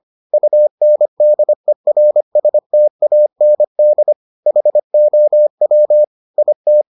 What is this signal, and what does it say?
Morse code, heard as a keyed tone, about 25 wpm, 600 Hz. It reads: UNDERSTAND HOW IT